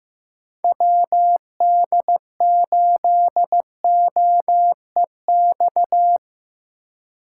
Morse code, heard as a keyed tone, about 17 wpm, 700 Hz. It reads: WD8OEX
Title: Speed Racing - USA Call Signs 15wpm